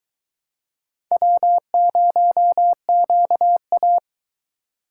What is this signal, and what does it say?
Morse code, heard as a keyed tone, about 22 wpm, 700 Hz. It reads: W0QA